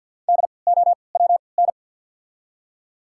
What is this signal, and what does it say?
Morse code, heard as a keyed tone, about 48 wpm, spaced as for 28 wpm, 700 Hz. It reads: DOWN
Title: 3x Character Spacing - Top 100 Words - 50wpm